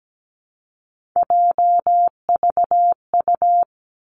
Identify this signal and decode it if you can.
Morse code, heard as a keyed tone, about 17 wpm, 700 Hz. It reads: JVU
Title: Three Challenging Letter-Number Combinations 17wpm